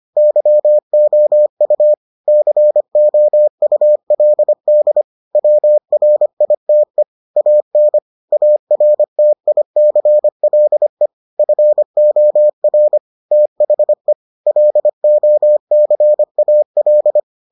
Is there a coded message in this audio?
Yes: Morse code from 0.2 to 17.2 s, heard as a keyed tone, about 25 wpm, 600 Hz. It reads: YOU COULD WRITE AN ARTICLE FOR THE LOCAL